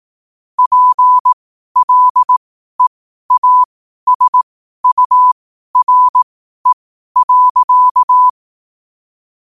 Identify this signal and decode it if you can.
Morse code, heard as a keyed tone, about 18 wpm, spaced as for 12 wpm, 1000 Hz. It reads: PLEASURE.
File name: qotdfiles / 12wpm